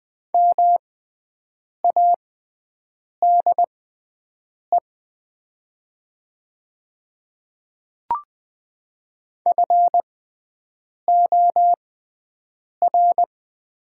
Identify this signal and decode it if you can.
Morse code, heard as a keyed tone, about 20 wpm, spaced as for 7 wpm, 700 Hz. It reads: MADE FOR
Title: 6x Character Spacing - Top 200 Words - 20wpm